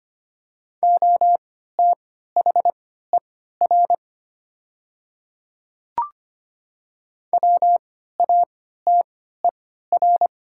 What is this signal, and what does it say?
Morse code, heard as a keyed tone, about 25 wpm, spaced as for 14 wpm, 700 Hz. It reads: OTHER WATER
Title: 3x Character Spacing - Top 100 Words - 25wpm